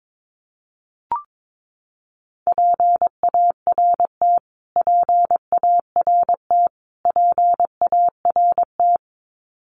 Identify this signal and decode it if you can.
Morse code, heard as a keyed tone, about 22 wpm, 700 Hz. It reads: PART PART PART
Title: Top 200 Words Repeated 22wpm